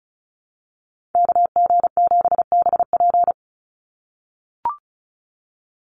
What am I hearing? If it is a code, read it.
Morse code, heard as a keyed tone, about 34 wpm, 700 Hz. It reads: KG7BP
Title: USA Call Signs 35wpm